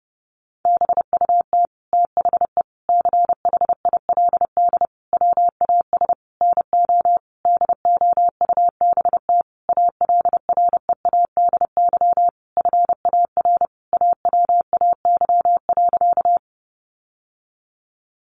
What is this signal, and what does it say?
Morse code, heard as a keyed tone, about 30 wpm, 700 Hz. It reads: BUT THE CHILD WAS NO DOUBT ALREADY FAR AWAY.
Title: Sentences from Top 1500 Words 30wpm